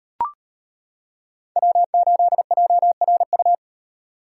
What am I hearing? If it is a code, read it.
Morse code, heard as a keyed tone, about 35 wpm, 700 Hz. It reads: W8JRU